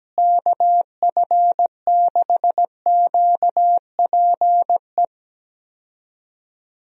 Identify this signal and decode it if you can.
Morse code, heard as a keyed tone, about 17 wpm, 700 Hz. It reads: KF6QPE
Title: USA Call Signs - 6 Characters - Repeated 3x - 17wpm